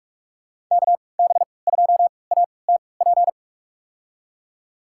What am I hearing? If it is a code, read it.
Morse code, heard as a keyed tone, about 47 wpm, spaced as for 26 wpm, 700 Hz. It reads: KB2ATP
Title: USA Call Signs - 6 Characters - 3x Character Spacing - 45wpm